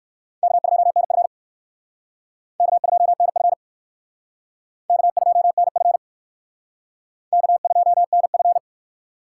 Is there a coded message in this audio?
Yes: Morse code from 0.4 to 8.6 s, heard as a keyed tone, about 60 wpm, 700 Hz. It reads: K2NF K2NF K2NF K2NF